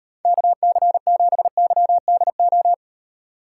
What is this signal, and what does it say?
Morse code, heard as a keyed tone, about 37 wpm, 700 Hz. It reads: KC7YDO